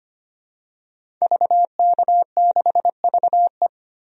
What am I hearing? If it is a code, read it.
Morse code, heard as a keyed tone, about 25 wpm, 700 Hz. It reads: VK6VE
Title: DXCC-100 Call Signs - 5 Characters - 25wpm